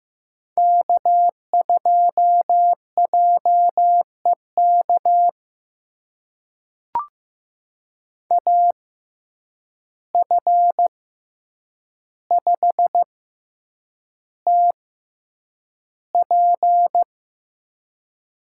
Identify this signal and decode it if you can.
Morse code, heard as a keyed tone, about 15 wpm, spaced as for 8 wpm, 700 Hz. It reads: K2JEK AF5TP
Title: USA Call Signs - 5 Characters - 6x Character Spacing - 15wpm